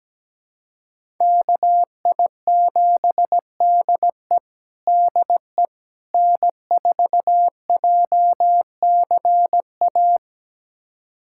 Morse code, heard as a keyed tone, about 17 wpm, 700 Hz. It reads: KI7DE DE N4JCA
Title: USA Callsign DE Callsign - 5 Characters - 17wpm